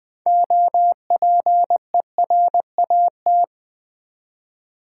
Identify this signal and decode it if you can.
Morse code, heard as a keyed tone, about 20 wpm, 700 Hz. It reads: OPERAT
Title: Build A Word - 3 Letters At A Time - 8-Letter Words - 20wpm